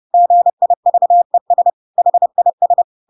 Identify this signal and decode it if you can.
Morse code, heard as a keyed tone, about 30 wpm, 700 Hz. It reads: GIVES HIS
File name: Pinocchio / Pinocchio ch7 30 30 ik1hge